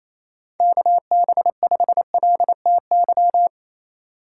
Morse code, heard as a keyed tone, about 27 wpm, 700 Hz. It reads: KB5LTY
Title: USA Call Signs - 4x Character Spacing - 28wpm